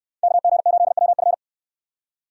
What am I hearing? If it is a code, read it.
Morse code, heard as a keyed tone, about 63 wpm, 700 Hz. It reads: KG8WF